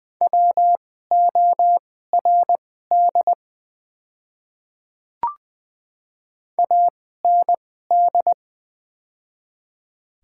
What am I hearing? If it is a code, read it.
Morse code, heard as a keyed tone, about 20 wpm, spaced as for 14 wpm, 700 Hz. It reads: WORD AND
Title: 2x Character Spacing - Top 300 Words - 20wpm